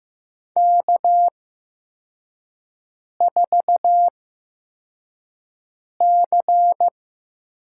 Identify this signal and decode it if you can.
Morse code, heard as a keyed tone, about 15 wpm, spaced as for 4 wpm, 700 Hz. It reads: K4C
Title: USA Call Signs - 5 Characters - 8x Character Spacing - 15wpm